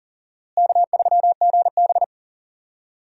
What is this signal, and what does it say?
Morse code, heard as a keyed tone, about 40 wpm, 700 Hz. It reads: K3GB
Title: USA Call Signs - 4 Characters - Repeated 3x - 40wpm